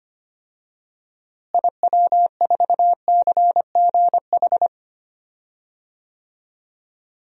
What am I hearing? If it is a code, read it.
Morse code, heard as a keyed tone, about 25 wpm, 700 Hz. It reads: IW4CGH
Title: DXCC-100 Call Signs - 6 Characters - Repeated 3x - 25wpm